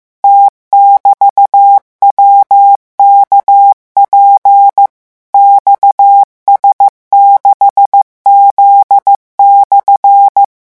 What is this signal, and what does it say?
Morse code, heard as a keyed tone, about 15 wpm, 800 Hz. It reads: T<BT>WKP XS6Z/